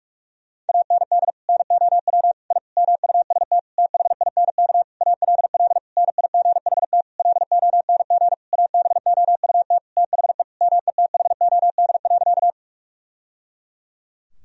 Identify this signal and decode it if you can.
Morse code, heard as a keyed tone, about 46 wpm, 700 Hz. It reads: AND NOW I MUST THINK ALL NIGHT LONG ABOUT THE METHOD.
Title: Sentences from Top 1000 Words - Review Part 3 - 45wpm